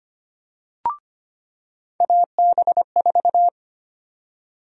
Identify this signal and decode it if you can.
Morse code, heard as a keyed tone, about 25 wpm, 700 Hz. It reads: AB4
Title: USA Call Sign Prefixes - 25wpm